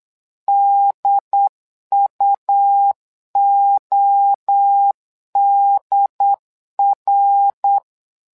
Morse code, heard as a keyed tone, about 8 wpm, 800 Hz. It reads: DUODR
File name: LEZIONI / 3 MWRO / 3 MWRO 8wpm 2606